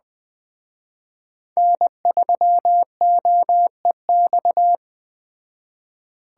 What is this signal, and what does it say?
Morse code, heard as a keyed tone, about 20 wpm, 700 Hz. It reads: N3OEX
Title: Speed Racing - USA Call Signs 17wpm